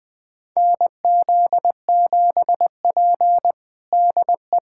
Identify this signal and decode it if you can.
Morse code, heard as a keyed tone, about 20 wpm, 700 Hz. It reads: NZ7P DE
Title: USA Callsign DE Callsign - 4 Characters - 20wpm